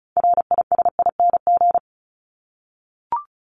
Morse code, heard as a keyed tone, about 34 wpm, 700 Hz. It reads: RISING
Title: ING Suffix - Top 100 Words with Base Words 35wpm